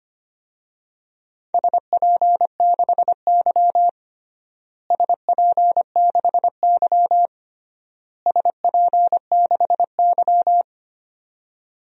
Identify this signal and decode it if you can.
Morse code, heard as a keyed tone, about 25 wpm, 700 Hz. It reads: SP6Y SP6Y SP6Y
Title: DXCC-100 Call Signs - 4 Characters - Repeated 3x - 25wpm